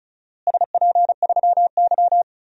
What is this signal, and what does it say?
Morse code, heard as a keyed tone, about 36 wpm, 700 Hz. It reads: SP3Y
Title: DXCC-100 Call Signs - 4 Characters - Repeated 3x - 35wpm